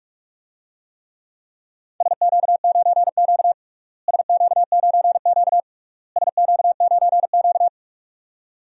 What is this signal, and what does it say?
Morse code, heard as a keyed tone, about 46 wpm, 700 Hz. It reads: SQ9Q SQ9Q SQ9Q
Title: DXCC-100 Call Signs - 4 Characters - Repeated 3x - 45wpm